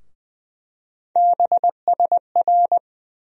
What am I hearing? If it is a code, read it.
Morse code, heard as a keyed tone, about 20 wpm, 700 Hz. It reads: BSR
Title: USA Call Sign Postfixes - 3 Letters 20wpm